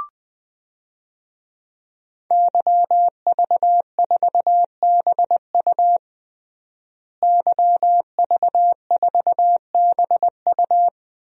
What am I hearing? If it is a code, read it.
Morse code, heard as a keyed tone, about 21 wpm, 700 Hz. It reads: YV4BU YV4BU
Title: DXCC-100 Call Signs - 5 Characters - Repeated 3x - 20wpm